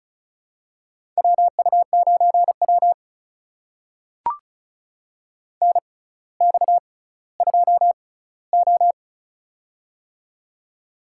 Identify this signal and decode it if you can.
Morse code, heard as a keyed tone, about 36 wpm, spaced as for 18 wpm, 700 Hz. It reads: WU9W NX2O
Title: USA Call Signs - 4 Characters - 6x Character Spacing - 35wpm